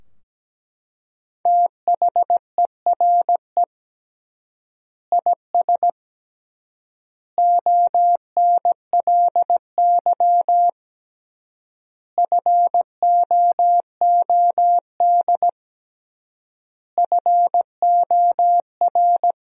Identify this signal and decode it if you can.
Morse code, heard as a keyed tone, about 17 wpm, 700 Hz. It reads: THERE IS ONLY FOOD FOR
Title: 3x Word-Spacing - Sentences from Top 300 Words - 17wpm